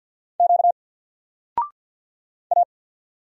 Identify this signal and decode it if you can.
Morse code, heard as a keyed tone, about 50 wpm, 700 Hz. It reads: Q A
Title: Single Letter-Number 50wpm